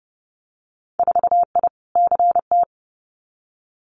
Morse code, heard as a keyed tone, about 30 wpm, 700 Hz. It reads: 4I CT
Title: Winter Field Day Exchanges Encore 30wpm